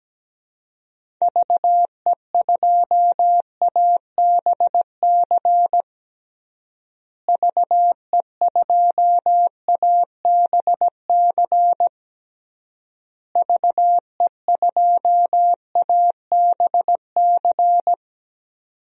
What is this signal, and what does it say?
Morse code, heard as a keyed tone, about 17 wpm, 700 Hz. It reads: VE2ABC VE2ABC VE2ABC
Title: DXCC-100 Call Signs - 6 Characters - Repeated 3x - 17wpm